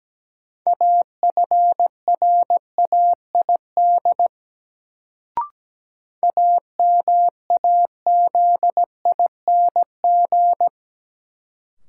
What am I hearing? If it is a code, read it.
Morse code, heard as a keyed tone, about 17 wpm, 700 Hz. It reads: AFRAID AMAZING
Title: Sentences from Top 1500 Words - New Word Drill - 17wpm